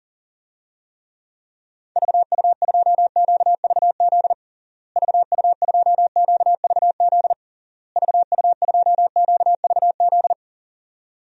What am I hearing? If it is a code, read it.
Morse code, heard as a keyed tone, about 40 wpm, 700 Hz. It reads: VU2QVZ VU2QVZ VU2QVZ
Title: DXCC-100 Call Signs - 6 Characters - Repeated 3x - 40wpm